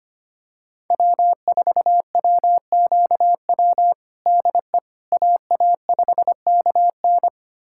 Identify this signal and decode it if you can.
Morse code, heard as a keyed tone, about 25 wpm, 700 Hz. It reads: W4WQW DE AA5KN
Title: USA Callsign DE Callsign - 5 Characters - 25wpm